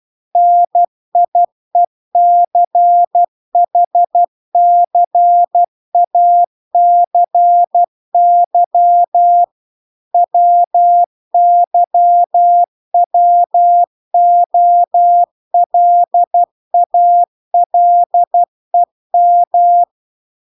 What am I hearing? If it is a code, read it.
Morse code, heard as a keyed tone, about 12 wpm, 700 Hz. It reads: NIECHCACY WYWOLALEM